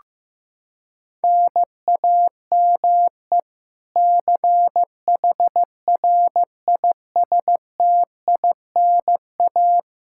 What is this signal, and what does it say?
Morse code, heard as a keyed tone, about 16 wpm, 700 Hz. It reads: NAME CHRISTINA
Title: K1USN SST QSO Practice 15wpm